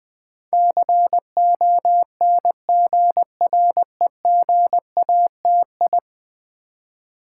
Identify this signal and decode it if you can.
Morse code, heard as a keyed tone, about 20 wpm, 700 Hz. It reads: CONGREGATI
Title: Build A Word - 5 Letters At A Time - 12-Letter Words - 20wpm